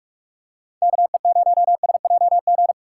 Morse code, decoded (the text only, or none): KE0SJG